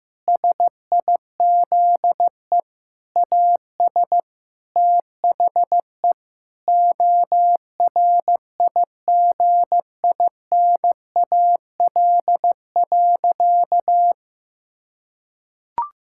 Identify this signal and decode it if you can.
Morse code, heard as a keyed tone, about 15 wpm, 700 Hz. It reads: SIZE AS THE ORIGINAL.
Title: Sentences from Top 1000 Words - Review Part 6 - 15wpm